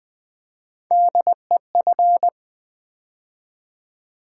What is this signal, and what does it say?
Morse code, heard as a keyed tone, about 20 wpm, 700 Hz. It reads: DEF
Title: Build A Word - 3 Letters At A Time - 8-Letter Words - 20wpm